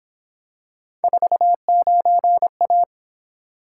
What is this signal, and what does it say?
Morse code, heard as a keyed tone, about 26 wpm, 700 Hz. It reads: 49A
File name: Speed-Racing - Three Letter-Number Combinations - 22wpm